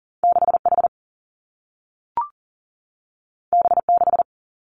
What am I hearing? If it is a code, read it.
Morse code, heard as a keyed tone, about 42 wpm, 700 Hz. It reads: 6 H B 6
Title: Two Challenging Letter-Number Combinations 40wpm